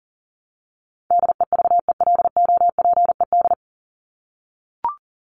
Morse code, heard as a keyed tone, about 40 wpm, 700 Hz. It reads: DEVELOPED
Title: ED Suffix - Top 100 Words with Base Words 40wpm